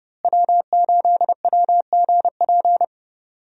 Morse code, heard as a keyed tone, about 32 wpm, 700 Hz. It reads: W8WGP
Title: USA Call Signs - 5 Characters - Repeated 3x - 30wpm